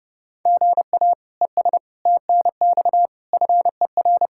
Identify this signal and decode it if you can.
Morse code, heard as a keyed tone, about 30 wpm, 700 Hz. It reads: GA ES TNX FER
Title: Practice QSO - Exchange 1 - Beginning - 30wpm